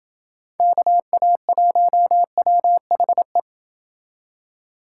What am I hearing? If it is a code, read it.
Morse code, heard as a keyed tone, about 28 wpm, 700 Hz. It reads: KA1WHE